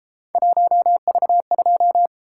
Morse code, heard as a keyed tone, about 30 wpm, 700 Hz. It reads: 1V2